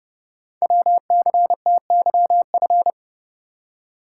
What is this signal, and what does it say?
Morse code, heard as a keyed tone, about 29 wpm, 700 Hz. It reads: WCTYF
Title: Code Groups - 5 Letter:Numbers - 30wpm